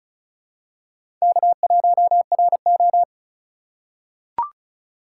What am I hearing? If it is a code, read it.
Morse code, heard as a keyed tone, about 36 wpm, 700 Hz. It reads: K1RO
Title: K1USN SST QSO Practice - 35wpm